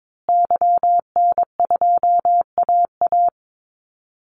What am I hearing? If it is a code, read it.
Morse code, heard as a keyed tone, about 22 wpm, 700 Hz. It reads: YN2AA
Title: Call Signs Repeated 3x - 22wpm